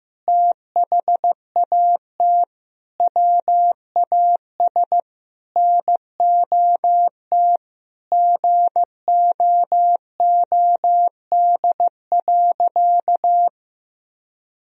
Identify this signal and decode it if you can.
Morse code, heard as a keyed tone, about 15 wpm, 700 Hz. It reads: THAT WAS NOT GOOD.
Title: Sentences from Top 1000 Words - Review Part 3 - 15wpm